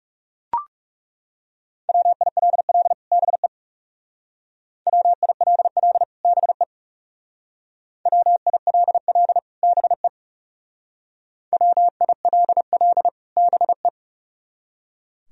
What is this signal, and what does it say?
Morse code, heard as a keyed tone, about 37 wpm, 700 Hz. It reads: WILL BE WILL BE WILL BE WILL BE
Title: Speed-Racing - Sets of 2 Words 30wpm